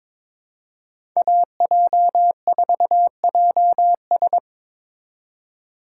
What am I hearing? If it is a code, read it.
Morse code, heard as a keyed tone, about 22 wpm, 690 Hz. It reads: AJ4JS
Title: USA Call Signs - 5 Characters - 2x Character Spacing - 22wpm